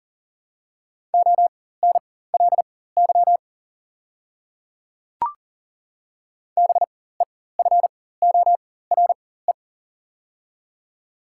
Morse code, heard as a keyed tone, about 37 wpm, spaced as for 18 wpm, 700 Hz. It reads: ONLY BEFORE
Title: 4x Character Spacing - Top 200 Words - 40wpm